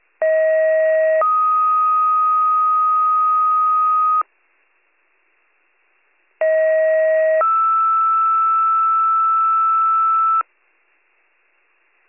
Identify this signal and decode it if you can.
Two-tone page, heard as a one-second tone, then a three-second tone, about 640 then 1250 Hz, about 640 then 1320 Hz. It is heard from 0.2 to 10.4 s.